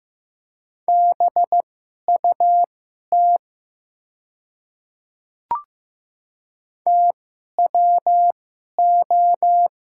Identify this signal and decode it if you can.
Morse code, heard as a keyed tone, about 15 wpm, spaced as for 11 wpm, 700 Hz. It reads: BUT TWO
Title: 2x Character Spacing - Top 200 Words - 15wpm